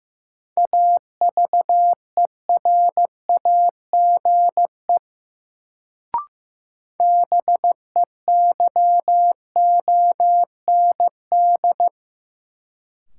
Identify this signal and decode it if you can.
Morse code, heard as a keyed tone, about 15 wpm, 700 Hz. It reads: AVERAGE BEYOND
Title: Sentences from Top 1000 Words - Review Part 6 - New Word Drill - 15wpm